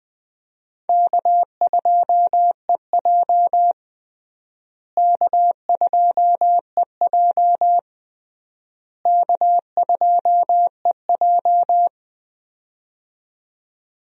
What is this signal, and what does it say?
Morse code, heard as a keyed tone, about 20 wpm, 700 Hz. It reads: K2EJ K2EJ K2EJ
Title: USA Call Signs - 4 Characters - Repeated 3x - 20wpm